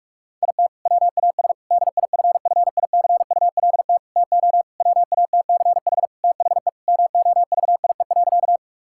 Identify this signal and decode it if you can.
Morse code, heard as a keyed tone, about 42 wpm, 700 Hz. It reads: IT WAS DIFFICULT TO WATCH THE MOVIE.
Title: Sentences from Top 1000 Words - Review Part 5 - 45wpm